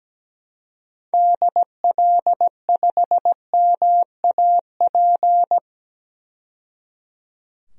Morse code, heard as a keyed tone, about 17 wpm, 700 Hz. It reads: DL5MAP
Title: DXCC-100 Call Signs - 6 Characters - 17wpm